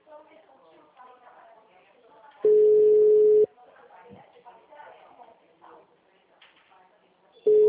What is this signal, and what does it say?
Ringback tone (Europe, 425 Hz), with speech.